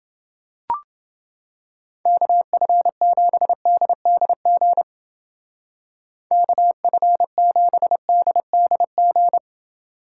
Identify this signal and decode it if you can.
Morse code, heard as a keyed tone, about 28 wpm, 700 Hz. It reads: KF7DDG KF7DDG